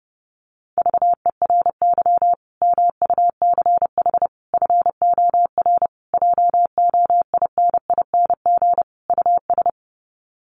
Morse code, heard as a keyed tone, about 28 wpm, 700 Hz. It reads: VERY MUCH FOR JOINING US
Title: Sets of 5 Words 30wpm